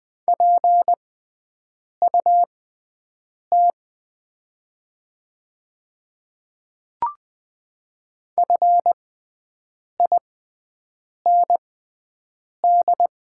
Morse code, heard as a keyed tone, about 20 wpm, spaced as for 7 wpm, 700 Hz. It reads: PUT FIND
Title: 6x Character Spacing - Top 200 Words - 20wpm